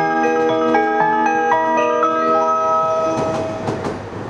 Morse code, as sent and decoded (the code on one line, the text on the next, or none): none